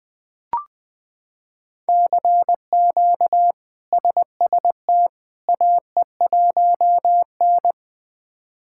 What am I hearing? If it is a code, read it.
Morse code, heard as a keyed tone, about 20 wpm, 700 Hz. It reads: CQ SST AE1N
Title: K1USN SST QSO Practice - 20wpm